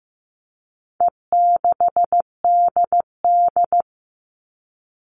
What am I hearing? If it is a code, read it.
Morse code, heard as a keyed tone, about 15 wpm, 700 Hz. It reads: E6DD
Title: All Cut Numbers - 4-Digit 15wpm